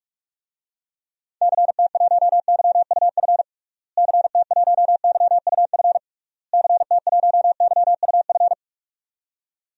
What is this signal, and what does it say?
Morse code, heard as a keyed tone, about 46 wpm, 700 Hz. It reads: CT1YUF CT1YUF CT1YUF
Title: DXCC-100 Call Signs - 6 Characters - Repeated 3x - 45wpm